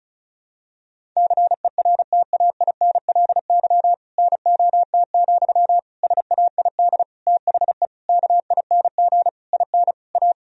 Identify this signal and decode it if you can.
Morse code, heard as a keyed tone, about 34 wpm, 700 Hz. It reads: CERTAINLY NOT, SAID THE KING IN A
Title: Sentences from Top 1000 Words - Review Part 1 - 35wpm